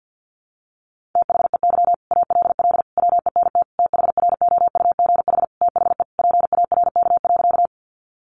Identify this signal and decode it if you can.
Morse code, heard as a keyed tone, about 50 wpm, 700 Hz. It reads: THEY ALL WENT THROUGH THE PARK.
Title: Sentences from Top 1300 Words 50wpm